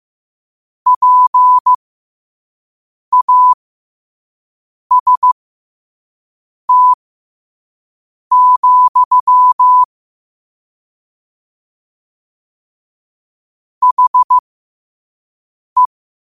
Morse code, heard as a keyed tone, about 15 wpm, spaced as for 5 wpm, 1000 Hz. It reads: PAST, HE
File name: qotdfiles / 5wpm